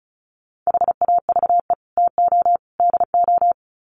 Morse code, heard as a keyed tone, about 34 wpm, 700 Hz. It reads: HAVE TO DO
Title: Sets of 3 Words 35wpm